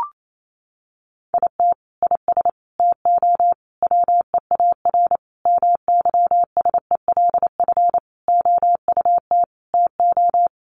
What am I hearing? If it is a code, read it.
Morse code, heard as a keyed tone, about 28 wpm, 700 Hz. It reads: IT IS TO WEAR MYSELF OUT TO
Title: Sentences from Top 1500 Words 28wpm